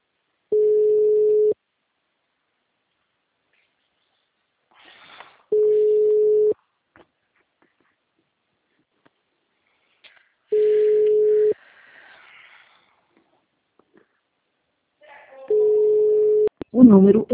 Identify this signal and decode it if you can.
Ringback tone (Europe, 425 Hz), with speech.
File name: out- -2003- - - .35240